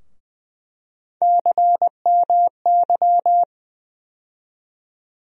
Morse code, heard as a keyed tone, about 20 wpm, 700 Hz. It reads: CMY
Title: USA Call Sign Postfixes - 3 Letters 20wpm